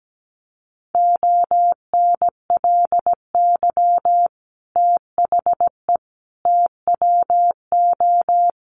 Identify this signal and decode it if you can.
Morse code, heard as a keyed tone, about 17 wpm, 700 Hz. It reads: ONLY THE TWO
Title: Sentences from Top 800 Words 17wpm